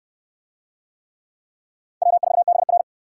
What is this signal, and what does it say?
Morse code, heard as a keyed tone, about 65 wpm, 700 Hz. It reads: W4BR